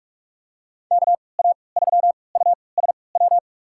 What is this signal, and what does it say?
Morse code, heard as a keyed tone, about 44 wpm, spaced as for 25 wpm, 700 Hz. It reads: KA3USW
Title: USA Call Signs - 6 Characters - 3x Character Spacing - 45wpm